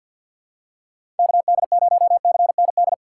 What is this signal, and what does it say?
Morse code, heard as a keyed tone, about 53 wpm, 700 Hz. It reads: KD0CND
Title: USA Call Signs - 6 Characters - 8x Character Spacing - 50wpm